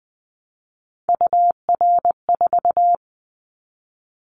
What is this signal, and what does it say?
Morse code, heard as a keyed tone, about 20 wpm, 700 Hz. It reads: UR4
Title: Callsign Prefixes for Top 25 DXCC - 20wpm